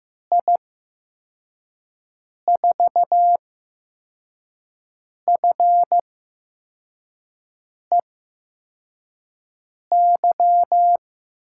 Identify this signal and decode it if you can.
Morse code, heard as a keyed tone, about 15 wpm, spaced as for 4 wpm, 700 Hz. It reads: I4FEY
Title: USA Call Signs - 6 Characters - 8x Character Spacing - 15wpm